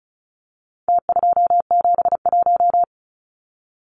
Morse code, heard as a keyed tone, about 34 wpm, 700 Hz. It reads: T271
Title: Numbers - Contest Style - 4-Digit 35wpm